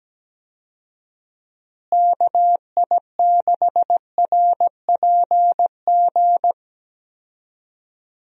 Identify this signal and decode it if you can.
Morse code, heard as a keyed tone, about 17 wpm, 700 Hz. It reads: KI6RPG
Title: USA Call Signs - 6 Characters - Repeated 2x - 17wpm